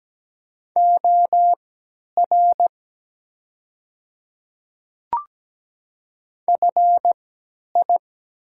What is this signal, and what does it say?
Morse code, heard as a keyed tone, about 17 wpm, spaced as for 10 wpm, 700 Hz. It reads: OR FI
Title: 3x Character Spacing - Top 500 Words - 17wpm